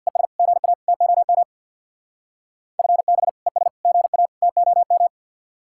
Morse code, heard as a keyed tone, about 64 wpm, 700 Hz. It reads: ES GA TOM FB ES GA TOM